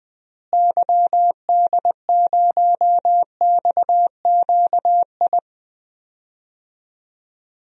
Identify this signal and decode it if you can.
Morse code, heard as a keyed tone, about 20 wpm, 700 Hz. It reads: YD0XQI
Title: DXCC-100 Call Signs - 6 Characters - Repeated 2x - 20wpm